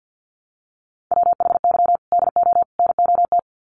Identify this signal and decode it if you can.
Morse code, heard as a keyed tone, about 50 wpm, 700 Hz. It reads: WHY DO NOT
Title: Sets of 3 Words 50wpm